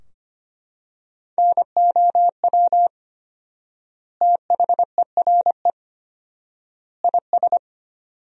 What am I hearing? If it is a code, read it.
Morse code, heard as a keyed tone, about 25 wpm, 700 Hz. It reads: NOW THERE IS
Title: 4x Word-Spacing - Sentences from Top 100 Words - 25wpm